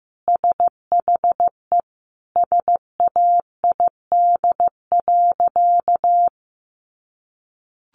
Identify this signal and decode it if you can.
Morse code, heard as a keyed tone, about 15 wpm, 700 Hz. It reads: SHE SAID.
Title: Sentences from Top 200 Words 15wpm